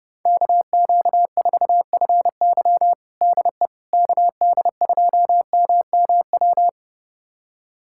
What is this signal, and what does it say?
Morse code, heard as a keyed tone, about 32 wpm, 700 Hz. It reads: KQ4FY DE KD2MMW
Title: USA Callsign DE Callsign 30wpm